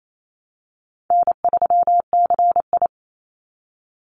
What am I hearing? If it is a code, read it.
Morse code, heard as a keyed tone, about 28 wpm, 700 Hz. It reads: N3CI
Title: Easy Call Signs Repeated 2x - 28wpm